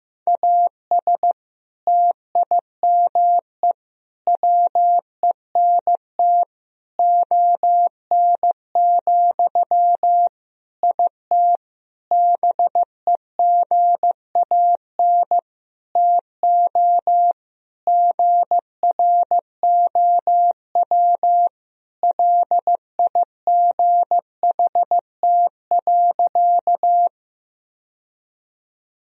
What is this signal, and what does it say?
Morse code, heard as a keyed tone, about 16 wpm, 700 Hz. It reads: AS TIME WENT ON, IT BEGAN TO GROW LIGHT.
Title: Sentences from Top 1000 Words - Review Part 1 - 15wpm